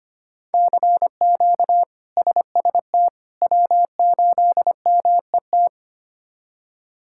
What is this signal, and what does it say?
Morse code, heard as a keyed tone, about 25 wpm, 700 Hz. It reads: CQ SST W8MET
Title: K1USN SST QSO Practice - 25wpm